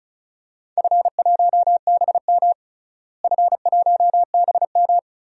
Morse code, heard as a keyed tone, about 36 wpm, 700 Hz. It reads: F1BM F1BM
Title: DXCC-100 Call Signs - 4 Characters - Repeated 3x - 35wpm